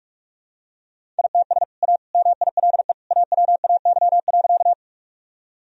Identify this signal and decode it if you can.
Morse code, heard as a keyed tone, about 45 wpm, 700 Hz. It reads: ITS A MILE AWAY.
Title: Sentences from Top 1000 Words - Review Part 1 - 45wpm